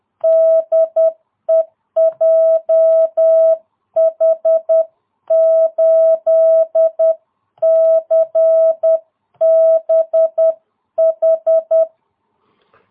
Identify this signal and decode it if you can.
Morse code, heard as a keyed tone, about 10 wpm, 660 Hz. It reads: DE JH8CBH